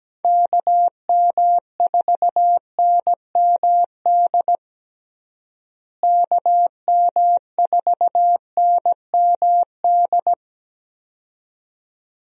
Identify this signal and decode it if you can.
Morse code, heard as a keyed tone, about 17 wpm, 700 Hz. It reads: KM4NMD KM4NMD
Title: USA Call Signs - 6 Characters - Repeated 2x - 17wpm